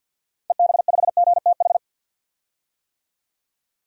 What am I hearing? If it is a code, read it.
Morse code, heard as a keyed tone, about 49 wpm, 700 Hz. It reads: EB5GTH
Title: DXCC-100 Call Signs - 6 Characters - 50wpm